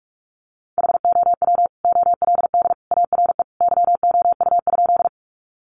Morse code, heard as a keyed tone, about 44 wpm, 700 Hz. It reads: HOW OLD ARE YOU?
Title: Sentences from Top 200 Words 45wpm